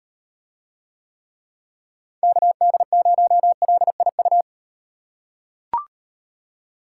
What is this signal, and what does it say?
Morse code, heard as a keyed tone, about 35 wpm, 700 Hz. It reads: KD0LIU